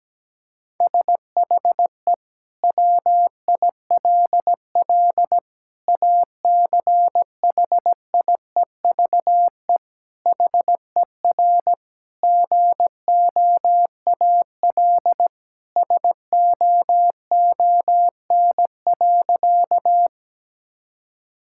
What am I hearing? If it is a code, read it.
Morse code, heard as a keyed tone, about 17 wpm, 700 Hz. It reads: SHE WILL ACHIEVE HER GOAL SOON.
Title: Sentences from Top 1000 Words - Review Part 6 - 17wpm